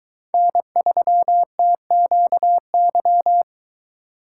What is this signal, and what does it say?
Morse code, heard as a keyed tone, about 23 wpm, 700 Hz. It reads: N3TQY